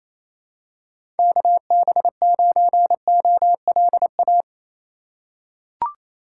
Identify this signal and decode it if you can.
Morse code, heard as a keyed tone, about 27 wpm, 700 Hz. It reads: KB9OLA
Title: USA Call Signs - 6 Characters - 3x Character Spacing - 28wpm